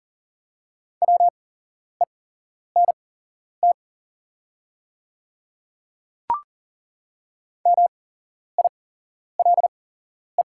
Morse code, heard as a keyed tone, about 39 wpm, spaced as for 11 wpm, 700 Hz. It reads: WENT MILE
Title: 8x Character Spacing - Top 500 Words - 40wpm